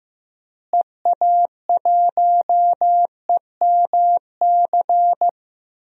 Morse code, heard as a keyed tone, about 16 wpm, 700 Hz. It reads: EA1EMC
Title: DXCC-100 Call Signs - 6 Characters - Repeated 3x - 15wpm